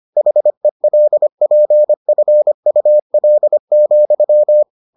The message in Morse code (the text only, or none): HELPFUL,